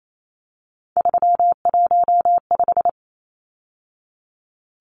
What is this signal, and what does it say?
Morse code, heard as a keyed tone, about 28 wpm, 700 Hz. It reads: 315
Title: Numbers - 3-Digit 28wpm